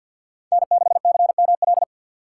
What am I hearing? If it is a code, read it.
Morse code, heard as a keyed tone, about 52 wpm, 700 Hz. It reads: N6CML
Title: USA Call Signs - 5 Characters - 3x Character Spacing - 50wpm